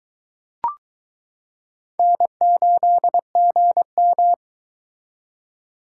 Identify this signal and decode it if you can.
Morse code, heard as a keyed tone, about 22 wpm, 700 Hz. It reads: N8GM